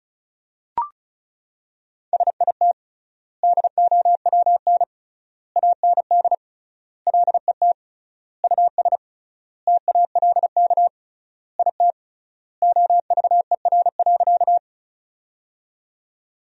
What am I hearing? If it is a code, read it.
Morse code, heard as a keyed tone, about 36 wpm, 700 Hz. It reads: SIT DOWN AND LET US TALK IT OVER.
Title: 3x Word-Spacing - Sentences from Top 500 Words - 35wpm